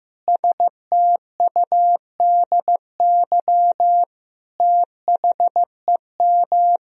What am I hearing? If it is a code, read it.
Morse code, heard as a keyed tone, about 15 wpm, 700 Hz. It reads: STUDY THEM
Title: Sentences from Top 300 Words and Abbreviations 15wpm